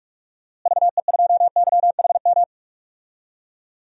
Morse code, heard as a keyed tone, about 44 wpm, 690 Hz. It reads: VE2YHM
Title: DXCC-100 Call Signs - 6 Characters - Repeated 2x - 45wpm